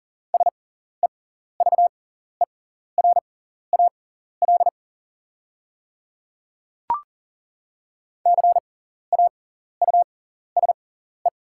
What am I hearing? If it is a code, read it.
Morse code, heard as a keyed tone, about 40 wpm, spaced as for 14 wpm, 690 Hz. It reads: SEVERAL CAUSE
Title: 6x Character Spacing - Top 500 Words - 40wpm